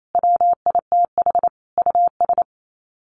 Morse code, heard as a keyed tone, about 28 wpm, 700 Hz. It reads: WITH US
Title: Sets of 5 Words 28wpm